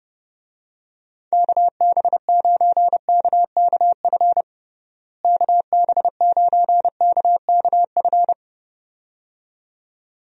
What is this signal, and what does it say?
Morse code, heard as a keyed tone, about 31 wpm, 700 Hz. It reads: KB9KKF KB9KKF
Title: USA Call Signs - 6 Characters - Repeated 2x - 30wpm